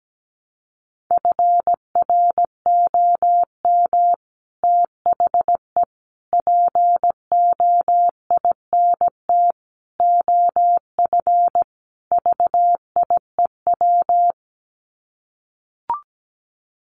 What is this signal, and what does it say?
Morse code, heard as a keyed tone, about 17 wpm, 700 Hz. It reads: FROM THE POINT OF VIEW
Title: Sets of 5 Words 17wpm